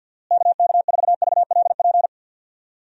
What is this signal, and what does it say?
Morse code, heard as a keyed tone, about 52 wpm, 700 Hz. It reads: KK4VLP
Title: USA Call Signs - 6 Characters - Repeated 3x - 50wpm